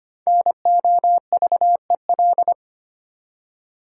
Morse code, heard as a keyed tone, about 25 wpm, 700 Hz. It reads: NOVEL
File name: 5 Letter Words - First 3 Letters Given - 25wpm